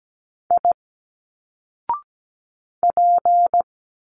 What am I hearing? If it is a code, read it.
Morse code, heard as a keyed tone, about 17 wpm, 700 Hz. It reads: I P
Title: Single Letters 17wpm